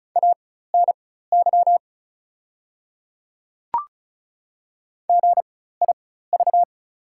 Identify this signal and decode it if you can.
Morse code, heard as a keyed tone, about 36 wpm, spaced as for 16 wpm, 690 Hz. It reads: ANY GIV
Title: 4x Character Spacing - Top 500 Words - 35wpm